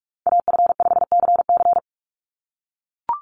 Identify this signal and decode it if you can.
Morse code, heard as a keyed tone, about 44 wpm, 700 Hz. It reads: AF5CC
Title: Call Signs - 45wpm